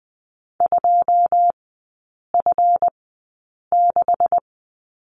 Morse code, heard as a keyed tone, about 20 wpm, spaced as for 8 wpm, 700 Hz. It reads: 2F6
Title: 152 Lesson 25e - Letter p - Three Characters 20wpm